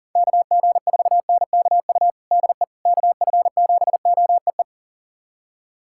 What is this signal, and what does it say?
Morse code, heard as a keyed tone, about 40 wpm, 700 Hz. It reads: KG4NKU DE KF7OEE
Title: USA Callsign DE Callsign - 6 Characters - 40wpm